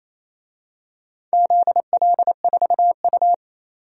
Morse code, heard as a keyed tone, about 28 wpm, 700 Hz. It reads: ZL4U
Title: DXCC-100 Call Signs - 4 Characters - 28wpm